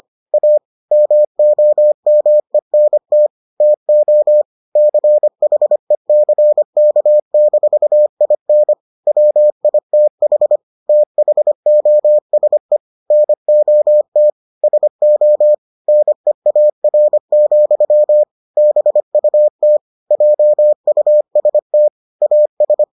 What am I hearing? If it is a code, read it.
Morse code, heard as a keyed tone, about 25 wpm, 600 Hz. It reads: A MOMENT TO CHECK-IN WITH THOSE NOT SO NEAR, BUT JUST AS